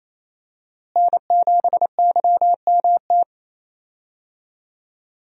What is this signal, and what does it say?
Morse code, heard as a keyed tone, about 28 wpm, 700 Hz. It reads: N7YMT
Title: USA Call Signs - 5 Characters - Repeated 2x - 28wpm